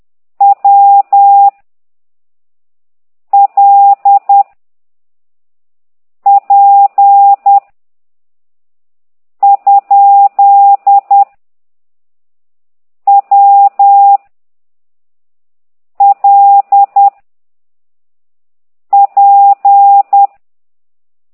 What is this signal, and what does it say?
Morse code, heard as a keyed tone, about 10 wpm, 800 Hz. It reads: W L P ? W L P
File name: ct2idw / exerc8